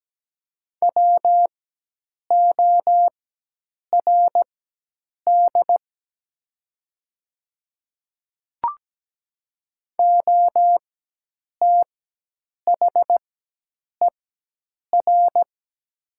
Morse code, heard as a keyed tone, about 17 wpm, spaced as for 8 wpm, 700 Hz. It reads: WORD OTHER
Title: 4x Character Spacing - Top 100 Words - 17wpm